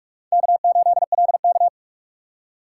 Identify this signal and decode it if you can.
Morse code, heard as a keyed tone, about 41 wpm, 700 Hz. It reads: K8LK
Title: USA Call Signs - 2x Character Spacing - 45wpm